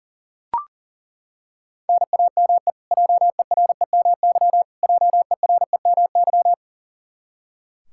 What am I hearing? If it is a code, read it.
Morse code, heard as a keyed tone, about 37 wpm, 700 Hz. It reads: NAME JEREMY JEREMY
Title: Practice QSO - Exchange 1 - Beginning - 40wpm